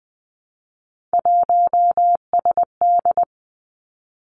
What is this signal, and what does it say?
Morse code, heard as a keyed tone, about 20 wpm, 700 Hz. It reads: 1SD
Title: Three Challenging Letter-Number Combinations 20wpm